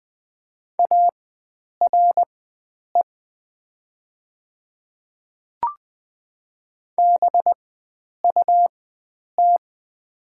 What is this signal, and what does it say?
Morse code, heard as a keyed tone, about 20 wpm, spaced as for 9 wpm, 700 Hz. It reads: ARE BUT
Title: 4x Character Spacing - Top 100 Words - 20wpm